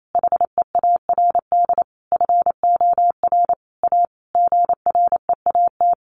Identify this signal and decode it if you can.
Morse code, heard as a keyed tone, about 28 wpm, 700 Hz. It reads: HEARD FOR A GREAT
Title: Sentences from Top 1200 Words 28wpm